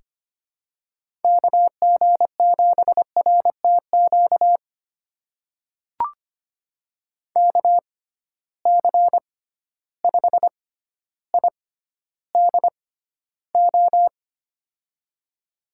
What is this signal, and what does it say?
Morse code, heard as a keyed tone, about 25 wpm, spaced as for 13 wpm, 700 Hz. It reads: KG7RTQ KC5IDO
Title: USA Call Signs - 6 Characters - 6x Character Spacing - 25wpm